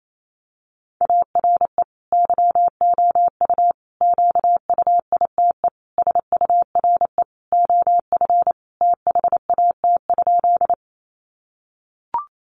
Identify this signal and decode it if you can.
Morse code, heard as a keyed tone, about 28 wpm, 700 Hz. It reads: ARE YOU QUITE SURE OF THAT?
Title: Sentences from Top 700 Words 28wpm